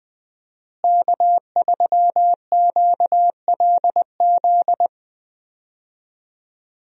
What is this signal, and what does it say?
Morse code, heard as a keyed tone, about 20 wpm, 700 Hz. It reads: K3QLZ
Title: USA Call Signs - 5 Characters - Repeated 2x - 20wpm